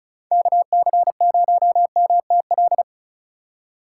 Morse code, heard as a keyed tone, about 36 wpm, 700 Hz. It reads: KC0MTL
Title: USA Call Signs - 3x Character Spacing - 35wpm